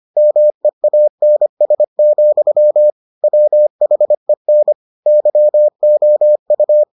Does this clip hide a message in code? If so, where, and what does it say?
Yes: Morse code from 0.2 to 6.8 s, heard as a keyed tone, about 25 wpm, 600 Hz. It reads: MEANS, WHEN YOU